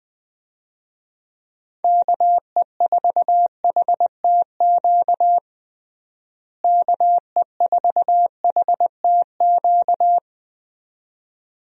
Morse code, heard as a keyed tone, about 21 wpm, 700 Hz. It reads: KE4HTQ KE4HTQ
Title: USA Call Signs - 6 Characters - Repeated 2x - 20wpm